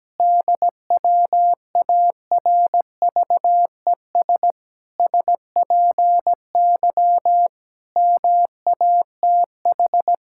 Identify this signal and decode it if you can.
Morse code, heard as a keyed tone, about 17 wpm, 700 Hz. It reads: DWARVES SPY MATH
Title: Pangram Challenge v2 - 17wpm